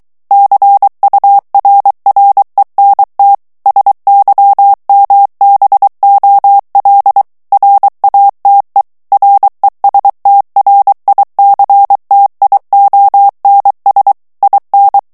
Morse code, heard as a keyed tone, about 23 wpm, 800 Hz. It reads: CURRENT SYMBOL RATE RESTRICTIONS IN
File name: ANGLAIS texte clair 3 vitesse 22 mots minute